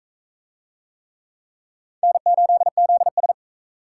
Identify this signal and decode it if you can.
Morse code, heard as a keyed tone, about 43 wpm, 700 Hz. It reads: N8ZS